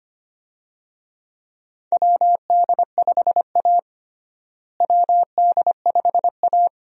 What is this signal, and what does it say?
Morse code, heard as a keyed tone, about 25 wpm, 700 Hz. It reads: WD5A WD5A
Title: USA Call Signs - 4 Characters - Repeated 3x - 25wpm